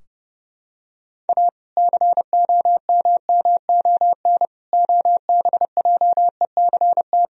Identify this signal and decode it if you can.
Morse code, heard as a keyed tone, about 31 wpm, 700 Hz. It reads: A COMMON OBJECT
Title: Sentences from Top 1000 Words - Review Part 3 - 30wpm